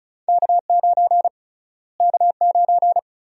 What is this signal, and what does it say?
Morse code, heard as a keyed tone, about 34 wpm, 700 Hz. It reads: K9 K9
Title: USA Call Sign Prefixes - Repeated 2x - 35wpm